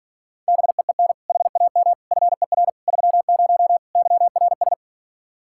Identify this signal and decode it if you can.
Morse code, heard as a keyed tone, about 43 wpm, 700 Hz. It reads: BEEN HAM FER 30 YRS
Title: Speed-Racing Common QSO Phrases - 35wpm